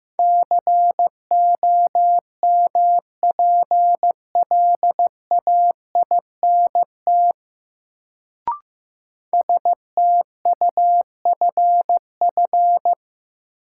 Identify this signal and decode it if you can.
Morse code, heard as a keyed tone, about 15 wpm, 700 Hz. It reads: COMPLAINT STUFF
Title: Sentences from Top 1400 Words - New Word Drill Alt - 15wpm